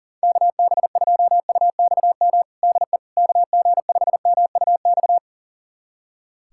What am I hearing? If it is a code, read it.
Morse code, heard as a keyed tone, about 40 wpm, 700 Hz. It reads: KB2UXM DE KG5MUX
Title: USA Callsign DE Callsign - 6 Characters - 40wpm